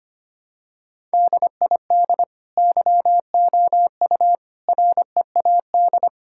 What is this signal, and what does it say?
Morse code, heard as a keyed tone, about 25 wpm, 700 Hz. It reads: DID YOU READ
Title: Sentences from Top 1000 Words - Review Part 5 - 25wpm